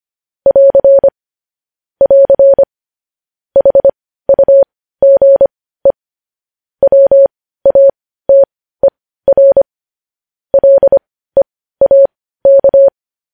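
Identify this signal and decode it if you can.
Morse code, heard as a keyed tone, about 25 wpm, spaced as for 15 wpm, 550 Hz. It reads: <AR> <AR> HUGE WATER LEAK
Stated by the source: Morse code news headlines broadcast.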